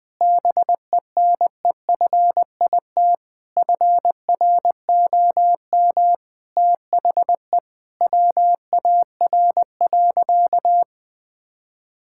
Morse code, heard as a keyed tone, about 20 wpm, 700 Hz. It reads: BENEFIT FROM THE WAR.
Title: Sentences from Top 1000 Words - Review Part 6 - 20wpm